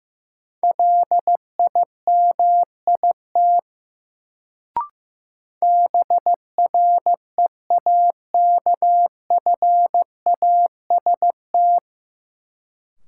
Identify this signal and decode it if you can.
Morse code, heard as a keyed tone, about 15 wpm, 700 Hz. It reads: LIMIT BREAKFAST
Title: Sentences from Top 1500 Words - New Word Drill - 15wpm